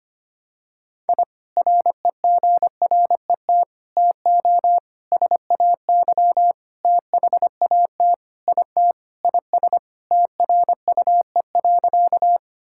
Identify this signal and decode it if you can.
Morse code, heard as a keyed tone, about 25 wpm, 700 Hz. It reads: I REGRET TO SAY THAT IT IS TRUE.
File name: practice / 25wpm / Sentences with 3-Word Phrases - 25wpm